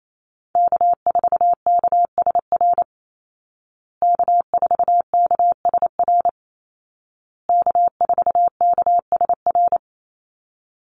Morse code, heard as a keyed tone, about 28 wpm, 700 Hz. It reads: K4KSR K4KSR K4KSR
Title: Call Signs Repeated 2x - 28wpm